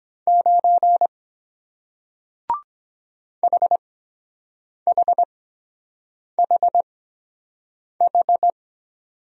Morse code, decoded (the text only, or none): 9 HHHH